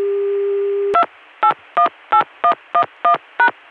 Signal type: DTMF